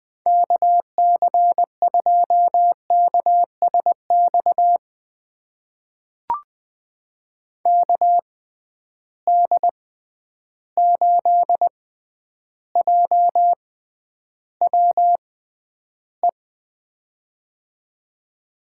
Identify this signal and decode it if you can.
Morse code, heard as a keyed tone, about 20 wpm, spaced as for 10 wpm, 700 Hz. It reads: KC2KSX KD8JWE
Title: USA Call Signs - 6 Characters - 6x Character Spacing - 20wpm